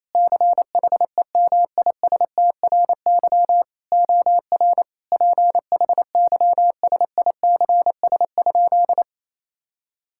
Morse code, heard as a keyed tone, about 28 wpm, 700 Hz. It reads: CHEMISTRY OR PHYSICS?
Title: Sentences from Top 1000 Words - Review Part 6 - 28wpm